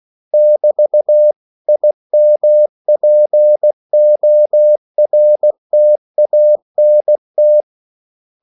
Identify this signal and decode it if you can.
Morse code, heard as a keyed tone, about 16 wpm, 600 Hz. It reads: <BT>IMPORTANT
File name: CW / Practice / 500MostCommonEnglishWords16